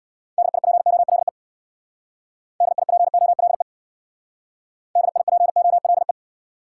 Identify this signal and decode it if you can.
Morse code, heard as a keyed tone, about 62 wpm, 700 Hz. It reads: DIPOLE DIPOLE DIPOLE
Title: Speed-Racing Top 180 QSO Elements 50wpm